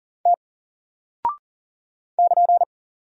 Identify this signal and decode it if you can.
Morse code, heard as a keyed tone, about 40 wpm, 700 Hz. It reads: T <KN>
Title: Single Character (Letter:Number:Punctuation:Prosign) 40wpm